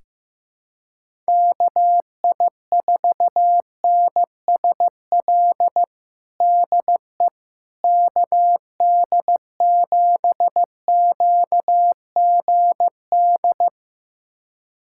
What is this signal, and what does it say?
Morse code, heard as a keyed tone, about 15 wpm, 700 Hz. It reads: KI4NSL DE KD7QGD
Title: USA Callsign DE Callsign - 6 Characters - 15wpm